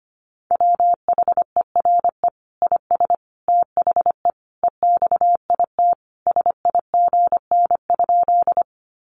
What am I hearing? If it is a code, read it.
Morse code, heard as a keyed tone, about 25 wpm, 700 Hz. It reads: WHERE IS THE EXIT SIGN?
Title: Sentences from Top 1100 Words 25wpm